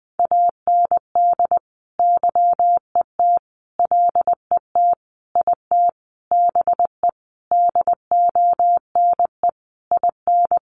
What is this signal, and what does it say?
Morse code, heard as a keyed tone, about 20 wpm, 700 Hz. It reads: AND YET LET IT BE DONE IN
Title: Sentences from Top 1200 Words 20wpm